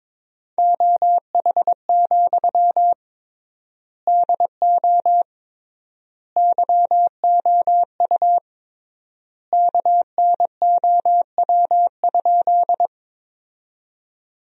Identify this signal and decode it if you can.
Morse code, heard as a keyed tone, about 22 wpm, 700 Hz. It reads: OH, DO YOU KNOW?
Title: 3x Word-Spacing - Sentences from Top 500 Words - 22wpm